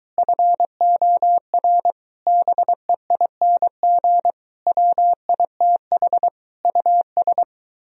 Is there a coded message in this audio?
Yes: Morse code from 0.2 to 7.4 s, heard as a keyed tone, about 23 wpm, 700 Hz. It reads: FOR BEING WITH US